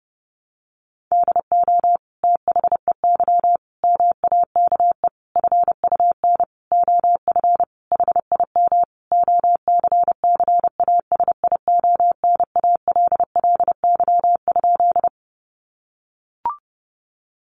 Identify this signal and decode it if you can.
Morse code, heard as a keyed tone, about 30 wpm, 700 Hz. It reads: DO THEY MAKE FUN OF HIM OCCASIONALLY?
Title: Sentences from Top 1300 Words 30wpm